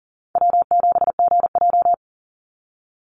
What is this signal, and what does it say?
Morse code, heard as a keyed tone, about 40 wpm, 700 Hz. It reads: W7GJ
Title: NA SOTA Chaser Callsigns 40wpm